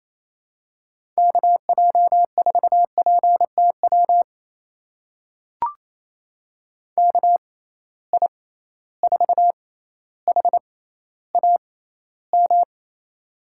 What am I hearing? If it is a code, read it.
Morse code, heard as a keyed tone, about 28 wpm, spaced as for 14 wpm, 700 Hz. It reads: KJ4PTW KI4HAM
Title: USA Call Signs - 6x Character Spacing - 28wpm